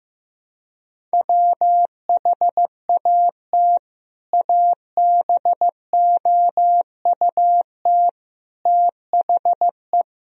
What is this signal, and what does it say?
Morse code, heard as a keyed tone, about 15 wpm, 700 Hz. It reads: WHAT ABOUT THE
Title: Sentences from Top 1000 Words - Review Part 2 - 15wpm